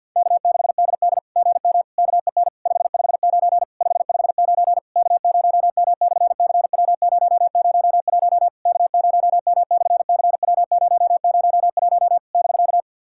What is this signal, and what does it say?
Morse code, heard as a keyed tone, about 50 wpm, 700 Hz. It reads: K6DD GM KEN 559 559 K0M/CW001 K0M/CW001 <BK>
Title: SOTA QSO Practice 50wpm